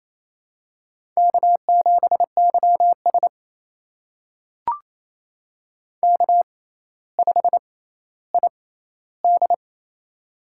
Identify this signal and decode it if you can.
Morse code, heard as a keyed tone, about 28 wpm, spaced as for 14 wpm, 700 Hz. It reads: K7YS K5ID
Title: USA Call Signs - 4 Characters - 6x Character Spacing - 28wpm